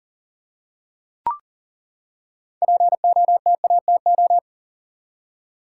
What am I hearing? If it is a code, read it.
Morse code, heard as a keyed tone, about 40 wpm, 700 Hz. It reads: POTATO
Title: Six Letter Words - 40wpm